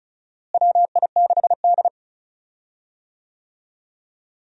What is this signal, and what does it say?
Morse code, heard as a keyed tone, about 36 wpm, 700 Hz. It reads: WI6D
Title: USA Call Signs - 4 Characters - Repeated 3x - 35wpm